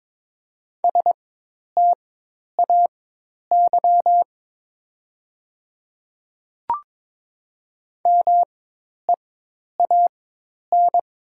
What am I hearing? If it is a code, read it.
Morse code, heard as a keyed tone, about 22 wpm, spaced as for 10 wpm, 700 Hz. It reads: STAY MEAN
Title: 4x Character Spacing - Top 500 Words - 22wpm